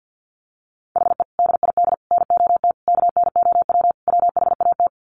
Morse code, heard as a keyed tone, about 50 wpm, 700 Hz. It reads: HE DID NOT KNOW WHAT
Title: Sets of 5 Words Encore 50wpm